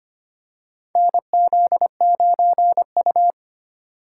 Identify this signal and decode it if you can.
Morse code, heard as a keyed tone, about 25 wpm, 700 Hz. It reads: NZ9U
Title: USA Call Signs - 3x Character Spacing - 25wpm